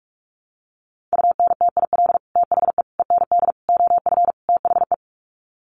Morse code, heard as a keyed tone, about 44 wpm, 700 Hz. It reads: UNTIL THE END OF THE
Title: Sets of 5 Words Encore 45wpm